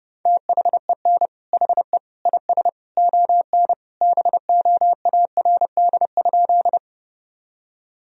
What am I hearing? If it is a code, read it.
Morse code, heard as a keyed tone, about 30 wpm, 700 Hz. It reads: THEN HE IS ON BOARD?
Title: Sentences from Top 1000 Words - Review Part 4 - 30wpm